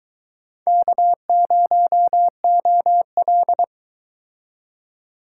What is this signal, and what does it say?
Morse code, heard as a keyed tone, about 22 wpm, 700 Hz. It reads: K0OL